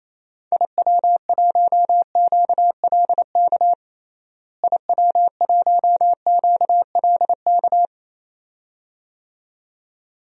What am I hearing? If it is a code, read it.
Morse code, heard as a keyed tone, about 28 wpm, 700 Hz. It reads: IW1QLK IW1QLK
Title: DXCC-100 Call Signs - 6 Characters - Repeated 2x - 28wpm